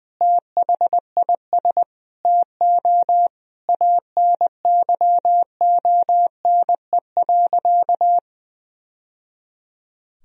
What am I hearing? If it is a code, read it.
Morse code, heard as a keyed tone, about 20 wpm, 700 Hz. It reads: THIS TO ANYONE.
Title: Sentences from Top 1000 Words - Review Part 2 - 20wpm